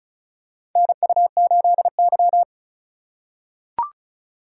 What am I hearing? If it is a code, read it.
Morse code, heard as a keyed tone, about 34 wpm, 700 Hz. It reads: NU8Y
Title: USA Call Signs - 4 Characters - 6x Character Spacing - 35wpm